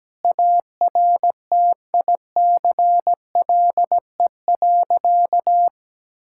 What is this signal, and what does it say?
Morse code, heard as a keyed tone, about 17 wpm, 700 Hz. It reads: ARTICLE.
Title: Sentences from Top 1000 Words - Review Part 5 - 17wpm